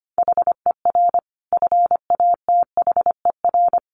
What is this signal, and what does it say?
Morse code, heard as a keyed tone, about 25 wpm, 700 Hz. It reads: HER FATHER
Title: Sentences from Top 600 Words 25wpm